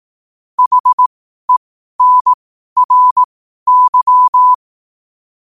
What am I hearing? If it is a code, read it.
Morse code, heard as a keyed tone, about 18 wpm, spaced as for 12 wpm, 1000 Hz. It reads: HENRY
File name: qotdfiles / 12wpm